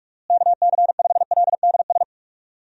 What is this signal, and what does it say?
Morse code, heard as a keyed tone, about 46 wpm, 700 Hz. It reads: KC5LDS
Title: USA Call Signs - 6 Characters - Repeated 3x - 45wpm